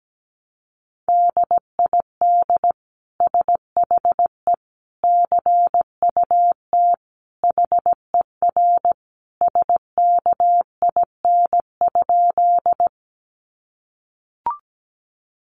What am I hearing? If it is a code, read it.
Morse code, heard as a keyed tone, about 17 wpm, 700 Hz. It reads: DID SHE CUT HER SKIN?
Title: Sentences from Top 1300 Words 17wpm